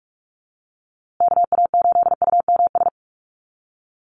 Morse code, heard as a keyed tone, about 44 wpm, 700 Hz. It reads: KA8UMS
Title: USA Call Signs Repeated 2x 45wpm